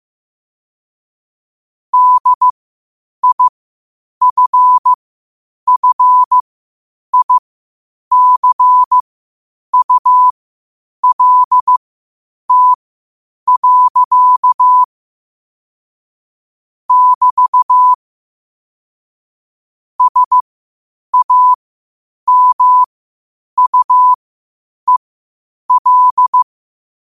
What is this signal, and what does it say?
Morse code, heard as a keyed tone, about 15 wpm, spaced as for 8 wpm, 1000 Hz. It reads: DIFFICULT. <BT> SAMUEL
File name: qotdfiles / 8wpm